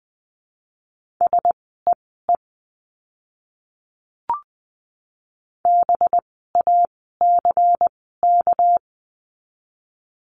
Morse code, heard as a keyed tone, about 20 wpm, spaced as for 15 wpm, 700 Hz. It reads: SEE BACK
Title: 237 Lesson 38f - Letter q - Words 20wpm